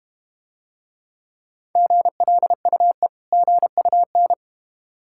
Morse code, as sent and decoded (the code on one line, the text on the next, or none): --. .-.. ..- . / --. ..- -.
GLUE GUN